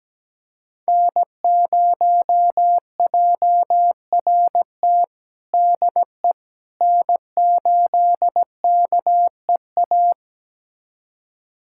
Morse code, heard as a keyed tone, about 17 wpm, 700 Hz. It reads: N0JRT DE N8KEA
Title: USA Callsign DE Callsign - 5 Characters - 17wpm